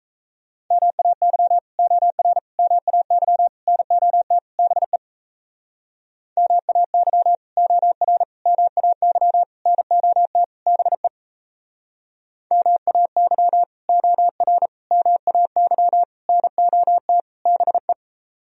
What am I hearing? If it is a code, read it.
Morse code, heard as a keyed tone, about 36 wpm, 700 Hz. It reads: MAY OR MAY NOT BE MAY OR MAY NOT BE MAY OR MAY NOT BE